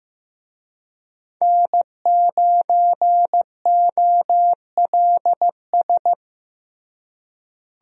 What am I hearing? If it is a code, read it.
Morse code, heard as a keyed tone, about 15 wpm, 700 Hz. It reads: N9OLS
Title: USA Call Signs - 5 Characters - Repeated 2x - 15wpm